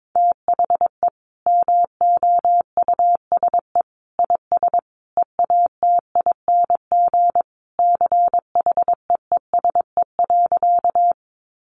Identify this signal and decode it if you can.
Morse code, heard as a keyed tone, about 22 wpm, 700 Hz. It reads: THE MOUSE IS EATING CHEESE.
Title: Easy Phrases 22wpm